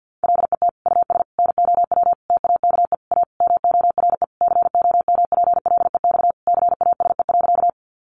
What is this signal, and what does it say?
Morse code, heard as a keyed tone, about 50 wpm, 700 Hz. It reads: LET US NOW TAKE A MORE COMPLEX CASE.
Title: Sentences from Top 900 Words 50wpm